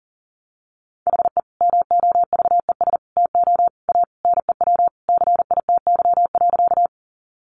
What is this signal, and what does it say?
Morse code, heard as a keyed tone, about 40 wpm, 700 Hz. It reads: HE MOVES TO A NEW CITY.
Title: Easy Phrases 40wpm